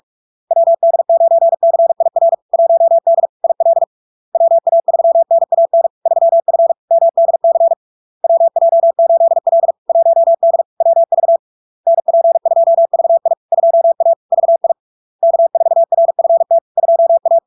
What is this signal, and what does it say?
Morse code, heard as a keyed tone, about 46 wpm, 670 Hz. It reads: WD9CIR 1D IL WA3NAN 3F MDC WJ8L 1D WV NP2VI 2A VI K4RFT 2A
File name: LARK-FD-2022-Log-in-CW-45wpm-670-Hz